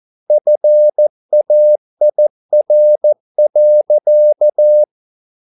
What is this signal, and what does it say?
Morse code, heard as a keyed tone, about 14 wpm, 600 Hz. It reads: FAIR.